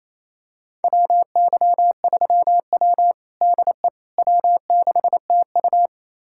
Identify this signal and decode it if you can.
Morse code, heard as a keyed tone, about 28 wpm, 700 Hz. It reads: WY3W DE W6TU
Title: USA Callsign DE Callsign - 4 Characters - 28wpm